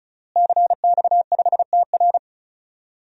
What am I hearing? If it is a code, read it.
Morse code, heard as a keyed tone, about 34 wpm, 700 Hz. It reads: CX5TR
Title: DXCC-100 Call Signs - 5 Characters - Repeated 3x - 35wpm